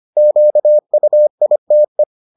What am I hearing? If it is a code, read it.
Morse code, heard as a keyed tone, about 25 wpm, 600 Hz. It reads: QUITE